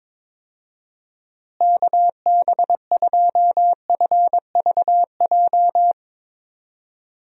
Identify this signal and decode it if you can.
Morse code, heard as a keyed tone, about 22 wpm, 700 Hz. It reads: KB2FVJ
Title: USA Call Signs - 6 Characters - Repeated 2x - 22wpm